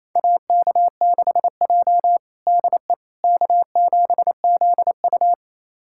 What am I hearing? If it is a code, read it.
Morse code, heard as a keyed tone, about 28 wpm, 700 Hz. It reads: AK6J DE K7ZU
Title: USA Callsign DE Callsign - 4 Characters - 28wpm